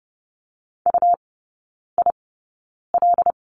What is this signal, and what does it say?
Morse code, heard as a keyed tone, about 30 wpm, spaced as for 9 wpm, 700 Hz. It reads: UIL
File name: Lesson 17e - Letter u - Three Characters 30wpm